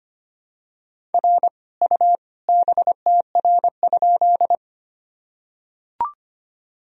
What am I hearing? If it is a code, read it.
Morse code, heard as a keyed tone, about 25 wpm, 700 Hz. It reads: R U BTR?
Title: Sentences from Top 400 Words and Abbreviations 25wpm